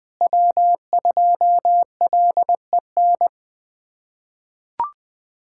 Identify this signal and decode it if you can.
Morse code, heard as a keyed tone, about 20 wpm, 700 Hz. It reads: W2LEN
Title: USA Callsign DE Callsign 20wpm